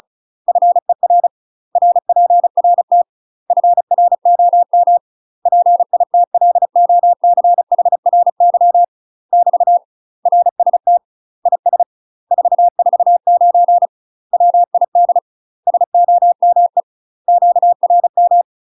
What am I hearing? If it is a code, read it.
Morse code, heard as a keyed tone, about 34 wpm, 700 Hz. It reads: FER RPRT FROM PITLOCHRY <BT> RST IS 449 WID SOME QRM
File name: UK MASTER QSOs TRAINING / 35WPM / UK MASTER QSOs 0043